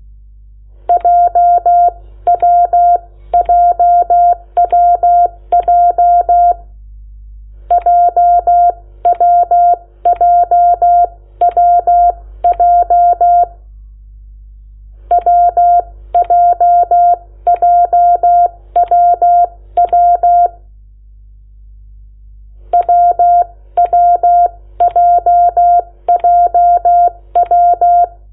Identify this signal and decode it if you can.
Morse code, heard as a keyed tone, about 16 wpm, spaced as for 13 wpm, 700 Hz. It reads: JWJWJ JWJWJ WJJWW WWJJW
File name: Lecciones / CLASE Nº8